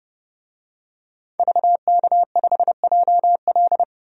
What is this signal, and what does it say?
Morse code, heard as a keyed tone, about 31 wpm, 700 Hz. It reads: VK5JL
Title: DXCC-100 Call Signs - 5 Characters - 30wpm